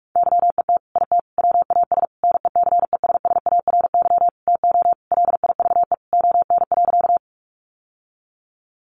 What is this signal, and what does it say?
Morse code, heard as a keyed tone, about 44 wpm, 700 Hz. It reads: YET IT WAS NECESSARY TO LIVE ON.
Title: Sentences from Top 800 Words 45wpm